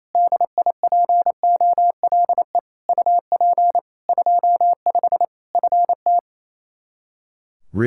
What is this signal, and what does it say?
Morse code, heard as a keyed tone, about 28 wpm, 700 Hz. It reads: DIPOLE UP 25 FT
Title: Practice QSO - Exchange 2 - Info - 28wpm